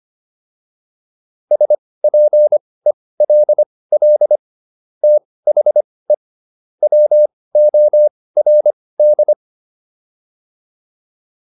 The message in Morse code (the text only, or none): SPELL THE WORD